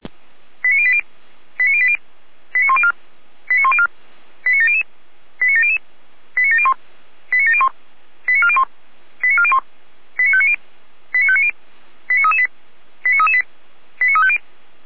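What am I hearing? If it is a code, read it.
Five-tone selective call (ZVEI), heard as quick runs of five tones, tones of about 70 ms. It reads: 89080 89080 89104 89104 89700 89700 89701 89701 89401 89401 89590 89590 89338 89338 89350
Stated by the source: ZVEI five-tone call 89358 received by SDR.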